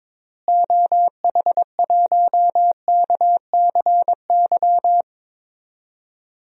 Morse code, heard as a keyed tone, about 22 wpm, 700 Hz. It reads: OH1KCY
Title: DXCC-100 Call Signs - 6 Characters - Repeated 3x - 22wpm